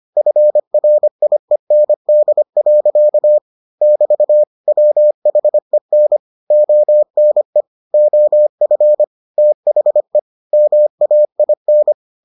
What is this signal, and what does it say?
Morse code, heard as a keyed tone, about 25 wpm, 600 Hz. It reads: FRIEND. <BT> WHEN ONE OF THE MAIN